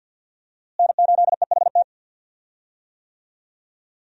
Morse code, heard as a keyed tone, about 46 wpm, 700 Hz. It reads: N8EHT
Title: USA Call Signs - 5 Characters - Repeated 3x - 50wpm